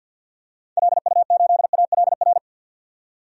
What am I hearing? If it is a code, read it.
Morse code, heard as a keyed tone, about 53 wpm, 700 Hz. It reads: LU8ALR
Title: DXCC-100 Call Signs - 6 Characters - 50wpm